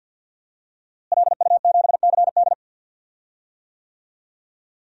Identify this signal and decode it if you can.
Morse code, heard as a keyed tone, about 51 wpm, 700 Hz. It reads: LU7CD
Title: DXCC-100 Call Signs - 5 Characters - Repeated 3x - 50wpm